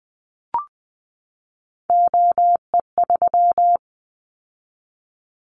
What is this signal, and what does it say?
Morse code, heard as a keyed tone, about 20 wpm, 700 Hz. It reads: OE3
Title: Callsign Prefixes for Top 25 DXCC - 20wpm